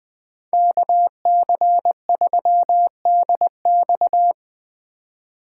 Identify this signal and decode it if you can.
Morse code, heard as a keyed tone, about 20 wpm, 700 Hz. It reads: KC3DX
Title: USA Call Signs - 5 Characters - 3x Character Spacing - 20wpm